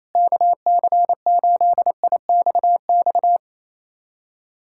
Morse code, heard as a keyed tone, about 28 wpm, 700 Hz. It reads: KC8IXX
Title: USA Callsign DE Callsign 28wpm